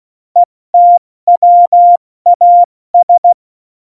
Morse code, heard as a keyed tone, about 16 wpm, 700 Hz. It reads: ETWAS
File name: Morsekurs / Witze / Witz 11. Juni 2019, 70 F80